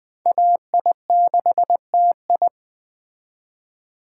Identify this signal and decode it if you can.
Morse code, heard as a keyed tone, about 20 wpm, 700 Hz. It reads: AI6TI